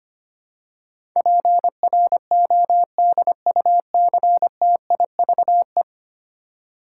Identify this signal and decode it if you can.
Morse code, heard as a keyed tone, about 25 wpm, 700 Hz. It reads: PRODUCTIVE
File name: 10 Letter Words - First 3 Letters Given - 25wpm